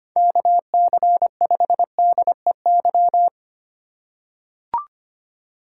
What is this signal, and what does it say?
Morse code, heard as a keyed tone, about 25 wpm, 700 Hz. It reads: KC5DEY
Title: USA Call Signs - 2x Character Spacing - 25wpm